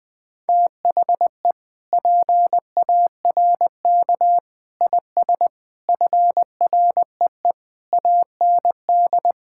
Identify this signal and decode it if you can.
Morse code, heard as a keyed tone, about 20 wpm, 700 Hz. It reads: THE PARK IS FREE AND
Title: Sentences with 7-Word Phrases - 20wpm